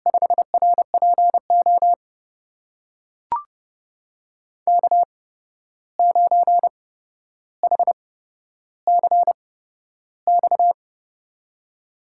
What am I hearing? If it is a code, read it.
Morse code, heard as a keyed tone, about 31 wpm, spaced as for 12 wpm, 700 Hz. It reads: 5RPO K9HCX
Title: USA Call Signs - 5 Characters - 8x Character Spacing - 30wpm